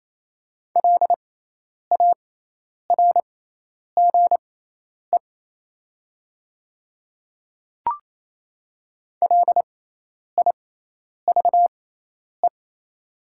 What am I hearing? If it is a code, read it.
Morse code, heard as a keyed tone, about 28 wpm, spaced as for 10 wpm, 700 Hz. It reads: LARGE LIVE
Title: 6x Character Spacing - Top 200 Words - 28wpm